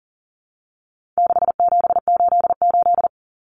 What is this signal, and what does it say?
Morse code, heard as a keyed tone, about 38 wpm, 700 Hz. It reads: 6788
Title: Numbers - 4-Digit 40wpm